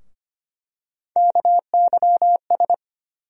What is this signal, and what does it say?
Morse code, heard as a keyed tone, about 25 wpm, 700 Hz. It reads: KYS
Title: USA Call Sign Postfixes - 3 Letters 25wpm